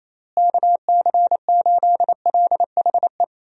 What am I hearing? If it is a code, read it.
Morse code, heard as a keyed tone, about 27 wpm, 700 Hz. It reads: KC8LHE
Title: USA Call Signs - 6 Characters - Repeated 2x - 28wpm